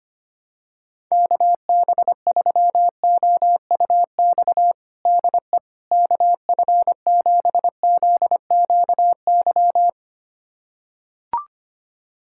Morse code, heard as a keyed tone, about 25 wpm, 700 Hz. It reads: KB3OUX DE KF7ZQY
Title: USA Callsign DE Callsign - 6 Characters - 25wpm